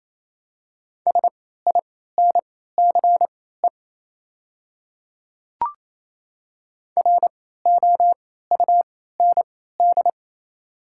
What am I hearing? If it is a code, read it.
Morse code, heard as a keyed tone, about 27 wpm, spaced as for 16 wpm, 700 Hz. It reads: SINCE ROUND
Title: 3x Character Spacing - Top 300 Words - 28wpm